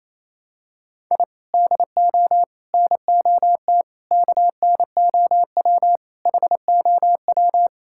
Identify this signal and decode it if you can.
Morse code, heard as a keyed tone, about 28 wpm, 700 Hz. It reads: I DO NOT KNOW HOW
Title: Sentences from Top 1000 Words - Review Part 6 - 28wpm